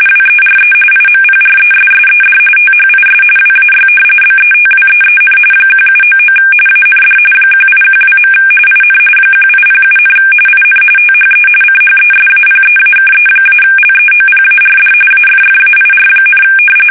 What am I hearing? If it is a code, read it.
RTTY, about 45.45 baud. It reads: WERE IMPRESSED THIS WAY BETWEEN 1799 AND 1812. IN 1807, IN THE CHESAPEAKE-LEOPARD AFFAIR,
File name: fsk850 / usnavy1